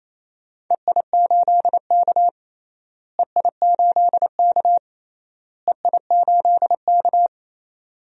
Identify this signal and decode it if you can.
Morse code, heard as a keyed tone, about 28 wpm, 700 Hz. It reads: EI8K EI8K EI8K
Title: DXCC-100 Call Signs - 4 Characters - Repeated 3x - 28wpm